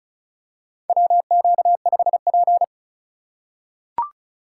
Morse code, heard as a keyed tone, about 36 wpm, 700 Hz. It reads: WQ5P
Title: USA Call Signs - 4 Characters - 3x Character Spacing - 35wpm